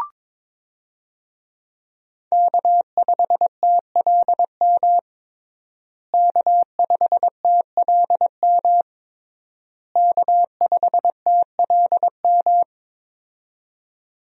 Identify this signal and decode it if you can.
Morse code, heard as a keyed tone, about 22 wpm, 700 Hz. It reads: K5TLM K5TLM K5TLM
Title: USA Call Signs - 5 Characters - Repeated 3x - 22wpm